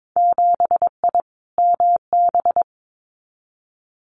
Morse code, heard as a keyed tone, about 22 wpm, 700 Hz. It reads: 7I MB
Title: Winter Field Day Exchanges Encore 22wpm